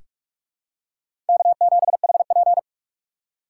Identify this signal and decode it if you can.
Morse code, heard as a keyed tone, about 47 wpm, 700 Hz. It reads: K7HP
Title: USA Call Signs - 4 Characters - 3x Character Spacing - 45wpm